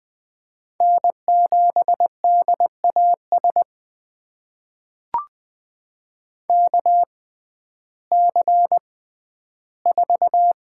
Morse code, heard as a keyed tone, about 20 wpm, spaced as for 12 wpm, 700 Hz. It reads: N7DAS KC4
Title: USA Call Signs - 6x Character Spacing - 20wpm